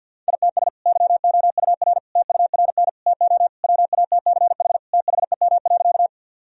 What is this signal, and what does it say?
Morse code, heard as a keyed tone, about 50 wpm, 700 Hz. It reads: ITS YOUR TURN TO WATCH THEM.
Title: Sentences from Top 1000 Words - Review Part 1 - 50wpm